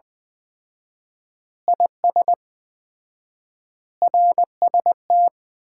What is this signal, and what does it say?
Morse code, heard as a keyed tone, about 20 wpm, 700 Hz. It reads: IS RST
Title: Common QSO Phrases - 4x Word-Spacing - 20wpm